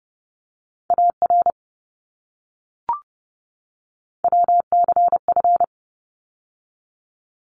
Morse code, heard as a keyed tone, about 30 wpm, 700 Hz. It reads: AR WCF
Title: ARRL Sections 30wpm